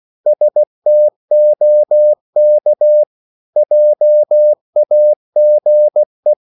Morse code, heard as a keyed tone, about 16 wpm, 600 Hz. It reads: STOK JAGE
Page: Korte danske ord 16wpm